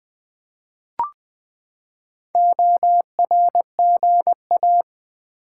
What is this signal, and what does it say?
Morse code, heard as a keyed tone, about 20 wpm, 700 Hz. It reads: ORGA
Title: Build A Word - 4 Letters At A Time - 12-Letter Words - 20wpm